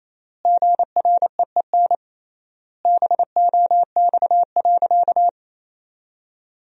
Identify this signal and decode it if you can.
Morse code, heard as a keyed tone, about 28 wpm, 700 Hz. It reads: GREEN BOX.
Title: 3x Word-Spacing - Sentences from Top 500 Words - 28wpm